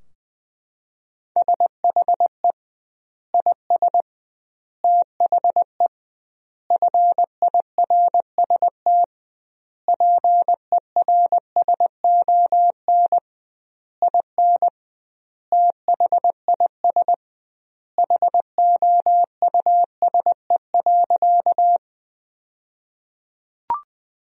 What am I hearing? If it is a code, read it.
Morse code, heard as a keyed tone, about 21 wpm, 700 Hz. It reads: SHE IS THE FIRST PERSON IN THIS HOUSE.
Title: 2x Word-Spacing - Sentences from Top 400 Words - 20wpm